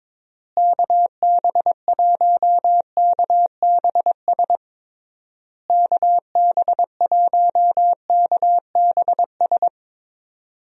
Morse code, heard as a keyed tone, about 22 wpm, 700 Hz. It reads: KB1KBS KB1KBS
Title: USA Call Signs - 6 Characters - Repeated 3x - 22wpm